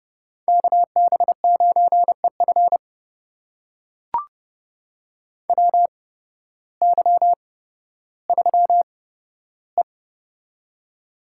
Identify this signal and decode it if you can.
Morse code, heard as a keyed tone, about 30 wpm, spaced as for 14 wpm, 700 Hz. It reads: KB9EF WY3E
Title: USA Call Signs - 8x Character Spacing - 30wpm